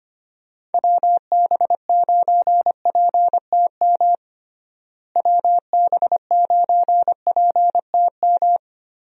Morse code, heard as a keyed tone, about 25 wpm, 700 Hz. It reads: WB9PTM WB9PTM
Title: USA Call Signs - 6 Characters - Repeated 3x - 25wpm